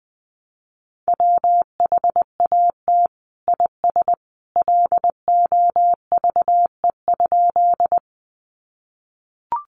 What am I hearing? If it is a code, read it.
Morse code, heard as a keyed tone, about 20 wpm, 700 Hz. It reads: WHAT IS LOVE?
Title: Sentences from Top 400 Words 20wpm